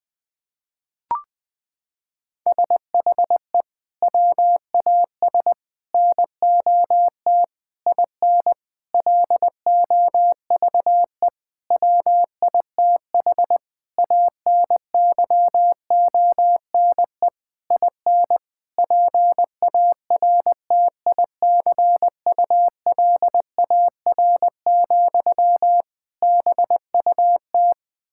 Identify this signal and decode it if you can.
Morse code, heard as a keyed tone, about 19 wpm, 700 Hz. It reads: SHE WAS NOT IN LOVE WITH ANYONE IN PARTICULAR, BUT
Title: Sentences from Top 1000 Words - Review Part 5 - 20wpm